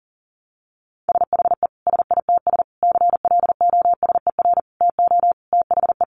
Morse code, heard as a keyed tone, about 40 wpm, 700 Hz. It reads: SHE SITS CLOSER TO THE
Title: Additional Easy Phrases 40wpm